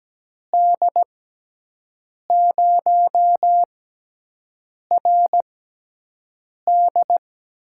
Morse code, heard as a keyed tone, about 17 wpm, spaced as for 6 wpm, 700 Hz. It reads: D0RD
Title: USA Call Signs - 6 Characters - 6x Character Spacing - 17wpm